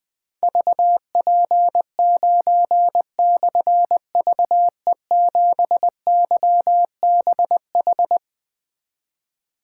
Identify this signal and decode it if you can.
Morse code, heard as a keyed tone, about 20 wpm, 700 Hz. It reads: VP9/VE7YBH
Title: Difficult Call Signs 20wpm